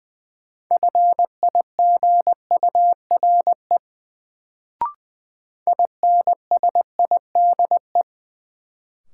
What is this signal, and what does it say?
Morse code, heard as a keyed tone, about 20 wpm, 700 Hz. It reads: FIGURE INSIDE
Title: Sentences from Top 1000 Words - Review Part 2 - New Word Drill - 20wpm